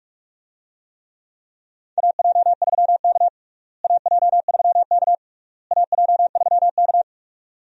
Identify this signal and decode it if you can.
Morse code, heard as a keyed tone, about 46 wpm, 700 Hz. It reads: AJ3K AJ3K AJ3K
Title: USA Call Signs - 4 Characters - Repeated 3x - 45wpm